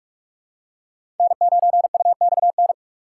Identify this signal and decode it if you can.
Morse code, heard as a keyed tone, about 46 wpm, 700 Hz. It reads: N9UXN
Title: USA Call Signs - 6x Character Spacing - 45wpm